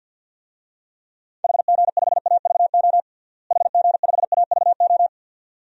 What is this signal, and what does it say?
Morse code, heard as a keyed tone, about 52 wpm, 700 Hz. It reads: HG5AVO HG5AVO
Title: DXCC-100 Call Signs - 6 Characters - Repeated 2x - 50wpm